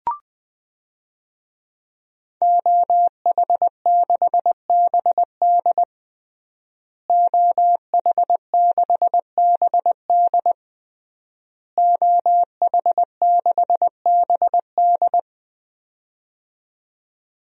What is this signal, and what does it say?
Morse code, heard as a keyed tone, about 21 wpm, 700 Hz. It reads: OH6BD OH6BD OH6BD
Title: DXCC-100 Call Signs - 5 Characters - Repeated 3x - 20wpm